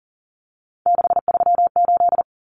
Morse code, heard as a keyed tone, about 38 wpm, 700 Hz. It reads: 638
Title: Numbers - 3-Digit 40wpm